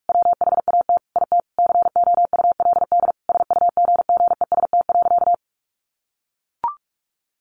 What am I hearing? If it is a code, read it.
Morse code, heard as a keyed tone, about 44 wpm, 700 Hz. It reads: WHAT IT COULD SUGGEST.
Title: Sentences from Top 1500 Words 45wpm